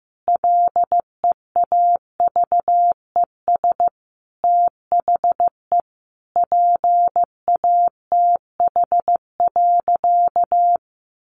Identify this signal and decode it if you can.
Morse code, heard as a keyed tone, about 15 wpm, 700 Hz. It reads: LEAVES THE PATH.
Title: A Sound Of Thunder 15wpm